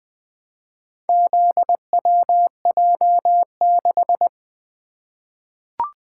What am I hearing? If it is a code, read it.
Morse code, heard as a keyed tone, about 20 wpm, 700 Hz. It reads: ZWJ6
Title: Four Letter-Number Combinations - 20wpm